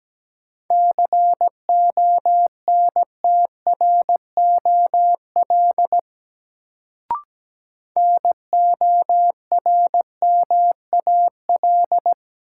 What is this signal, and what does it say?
Morse code, heard as a keyed tone, about 17 wpm, 700 Hz. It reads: CONTROL NORMAL
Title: Sentences from Top 1200 Words - New Word Drill Alt - 17wpm